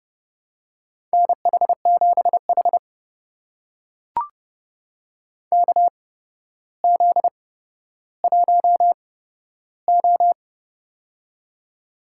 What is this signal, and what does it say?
Morse code, heard as a keyed tone, about 32 wpm, spaced as for 13 wpm, 700 Hz. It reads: NH7H KZ1O
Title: USA Call Signs - 4 Characters - 8x Character Spacing - 30wpm